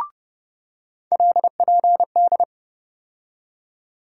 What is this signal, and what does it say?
Morse code, heard as a keyed tone, about 30 wpm, 700 Hz. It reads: LPD
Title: USA Call Sign Postfixes - 3 Letters 30wpm